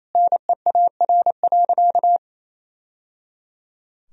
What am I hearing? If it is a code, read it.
Morse code, heard as a keyed tone, about 28 wpm, 700 Hz. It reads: NEAR.
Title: Sentences from Top 1000 Words - Review Part 1 - 28wpm